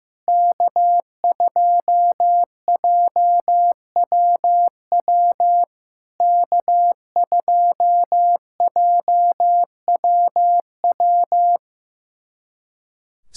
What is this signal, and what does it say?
Morse code, heard as a keyed tone, about 15 wpm, 700 Hz. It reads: K2JWW K2JWW
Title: SOTA QSO Practice 15wpm